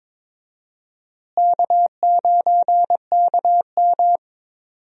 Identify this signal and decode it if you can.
Morse code, heard as a keyed tone, about 22 wpm, 700 Hz. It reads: K9KM
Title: USA Call Signs - 4 Characters - Repeated 3x - 22wpm